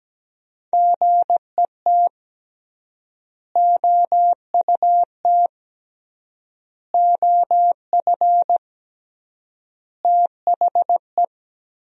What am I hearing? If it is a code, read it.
Morse code, heard as a keyed tone, about 17 wpm, 700 Hz. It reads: GET OUT OF THE
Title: 3x Word-Spacing - Sentences from Top 200 Words - 17wpm